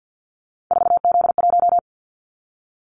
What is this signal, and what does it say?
Morse code, heard as a keyed tone, about 50 wpm, 700 Hz. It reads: 4Z1
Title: Callsign Prefixes for Top 50 DXCC - 50wpm